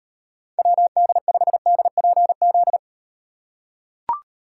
Morse code, heard as a keyed tone, about 38 wpm, 700 Hz. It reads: WD5DPZ